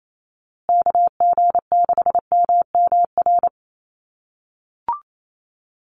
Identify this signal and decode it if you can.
Morse code, heard as a keyed tone, about 28 wpm, 700 Hz. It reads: KG6MMR
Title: USA Call Signs Repeated 3x 28wpm